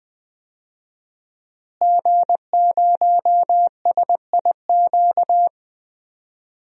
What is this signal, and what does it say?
Morse code, heard as a keyed tone, about 20 wpm, 700 Hz. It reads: G0SIQ
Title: DXCC-100 Call Signs - 5 Characters - Repeated 2x - 20wpm